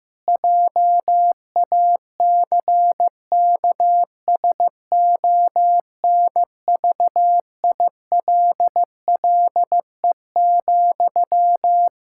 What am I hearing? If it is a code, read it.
Morse code, heard as a keyed tone, about 15 wpm, 700 Hz. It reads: JACKSONVILLE,
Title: Practice QSO - Exchange 1 - Beginning - 15wpm